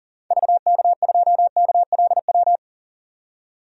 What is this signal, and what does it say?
Morse code, heard as a keyed tone, about 40 wpm, 700 Hz. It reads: VK2KLW
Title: DXCC-100 Call Signs - 6 Characters - 40wpm